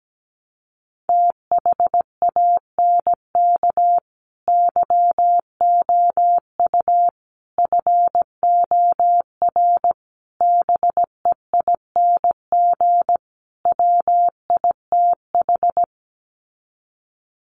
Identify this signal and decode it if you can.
Morse code, heard as a keyed tone, about 17 wpm, 700 Hz. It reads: THANK YOU FOR BEING WITH
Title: Sets of 5 Words 17wpm